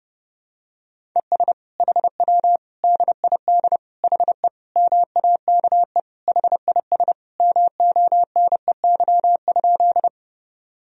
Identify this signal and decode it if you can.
Morse code, heard as a keyed tone, about 30 wpm, 700 Hz. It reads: ES HW DID HE MAKE HIS MONEY?
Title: Sentences from Top 400 Words and Abbreviations 30wpm